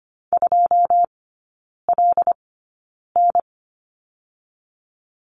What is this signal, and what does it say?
Morse code, heard as a keyed tone, about 25 wpm, spaced as for 9 wpm, 690 Hz. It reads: 2LN
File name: Lesson 14e - Number 2 - Three Characters 25wpm